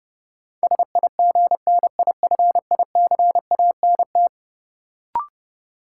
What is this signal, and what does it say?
Morse code, heard as a keyed tone, about 30 wpm, 700 Hz. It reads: SIGNIFICANT
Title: Sentences from Top 1600 Words - New Word Drill Alt - 30wpm